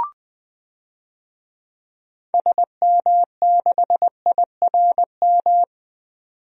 Morse code, heard as a keyed tone, about 21 wpm, 700 Hz. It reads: SM6IRM
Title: DXCC-100 Call Signs - 6 Characters - 20wpm